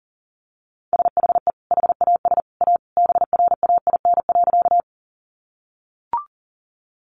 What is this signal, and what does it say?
Morse code, heard as a keyed tone, about 40 wpm, 700 Hz. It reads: SHE HAS A BRAIN.
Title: Sentences from Top 1100 Words 40wpm